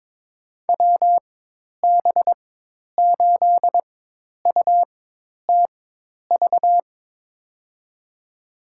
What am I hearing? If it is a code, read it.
Morse code, heard as a keyed tone, about 22 wpm, spaced as for 10 wpm, 700 Hz. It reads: WB8UTV
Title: USA Call Signs - 6 Characters - 4x Character Spacing - 22wpm